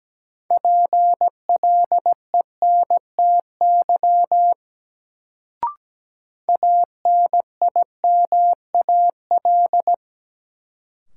Morse code, heard as a keyed tone, about 17 wpm, 700 Hz. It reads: PLENTY ANIMAL
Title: Sentences from Top 1300 Words - New Word Drill Alt - 17wpm